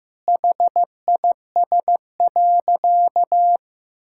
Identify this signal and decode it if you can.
Morse code, heard as a keyed tone, about 15 wpm, 700 Hz. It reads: HIS.
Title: Sentences from Top 1000 Words - Review Part 6 - 15wpm